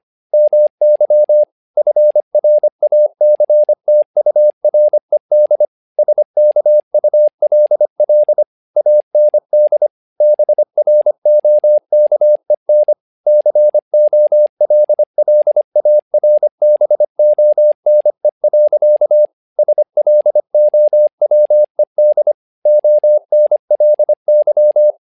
Morse code, heard as a keyed tone, about 25 wpm, 600 Hz. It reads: MY FRACTURED SKULL AND BROKEN COLLARBONE. SLOWED ONLY
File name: learning center / tower trouble / tower trouble 25wpm